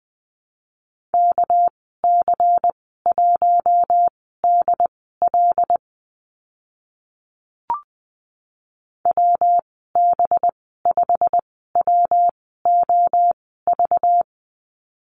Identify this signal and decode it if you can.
Morse code, heard as a keyed tone, about 20 wpm, spaced as for 14 wpm, 700 Hz. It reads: KC1DL WB5WOV
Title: 207 Lesson 33g - Letter j - Call Signs 20wpm